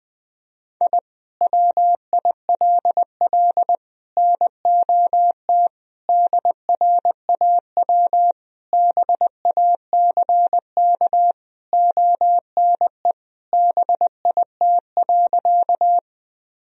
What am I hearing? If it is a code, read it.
Morse code, heard as a keyed tone, about 21 wpm, 700 Hz. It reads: I WILL NOT DRAW BACK ONE BIT.
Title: Sentences from Top 1000 Words - Review Part 1 - 20wpm